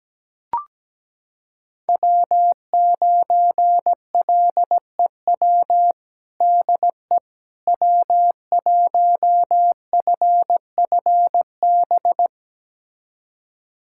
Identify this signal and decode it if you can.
Morse code, heard as a keyed tone, about 17 wpm, 700 Hz. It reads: W9LEW DE W1FFB
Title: USA Callsign DE Callsign - 5 Characters - 17wpm